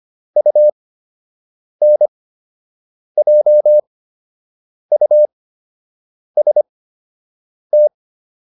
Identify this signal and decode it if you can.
Morse code, heard as a keyed tone, about 25 wpm, spaced as for 7 wpm, 600 Hz. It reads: UNJUST